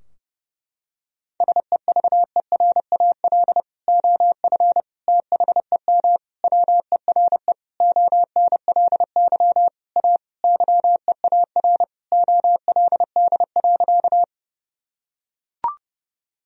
Morse code, heard as a keyed tone, about 30 wpm, 700 Hz. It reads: SEVERAL OF THEM WERE ONLY A YEAR OLD.
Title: Sentences from Top 1000 Words - Review Part 2 - 30wpm